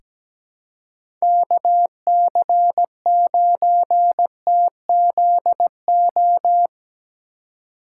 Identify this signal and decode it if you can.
Morse code, heard as a keyed tone, about 17 wpm, 700 Hz. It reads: KC9TZO
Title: K1USN SST QSO Practice - 17wpm